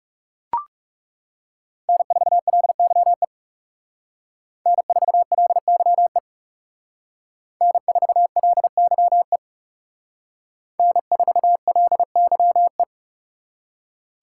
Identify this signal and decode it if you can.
Morse code, heard as a keyed tone, about 39 wpm, 700 Hz. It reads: N4LYE N4LYE N4LYE N4LYE